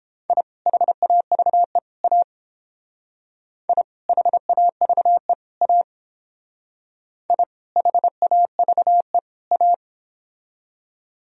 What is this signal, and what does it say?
Morse code, heard as a keyed tone, about 28 wpm, 700 Hz. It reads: I HAVE A I HAVE A I HAVE A